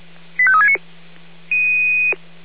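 Five-tone selective call (ZVEI), heard as quick runs of five tones, tones of about 70 ms. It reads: 85369
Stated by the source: ZVEI five-tone selective call received by SDR.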